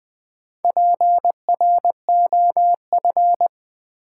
Morse code, heard as a keyed tone, about 20 wpm, 700 Hz. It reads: PROF
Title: Build A Word - 4 Letters At A Time - 12-Letter Words - 20wpm